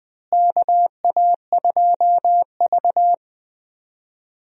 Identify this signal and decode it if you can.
Morse code, heard as a keyed tone, about 20 wpm, 700 Hz. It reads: KA2V
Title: USA Call Signs - 4 Characters - Repeated 3x - 20wpm